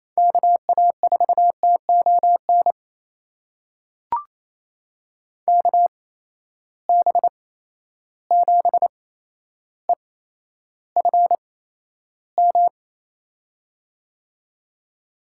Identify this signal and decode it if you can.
Morse code, heard as a keyed tone, about 27 wpm, spaced as for 12 wpm, 700 Hz. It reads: KA4TON KB7EFM
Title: USA Call Signs - 6 Characters - 8x Character Spacing - 28wpm